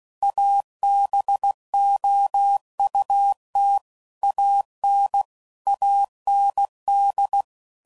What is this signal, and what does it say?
Morse code, heard as a keyed tone, about 16 wpm, 800 Hz. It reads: ABOUT AN AND
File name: CW Code Groups @15WPM / Technical Subjects / AND Gate